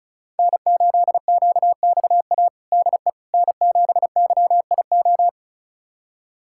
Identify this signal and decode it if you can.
Morse code, heard as a keyed tone, about 36 wpm, 700 Hz. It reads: N8QXA DE N7YIO
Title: USA Callsign DE Callsign - 5 Characters - 35wpm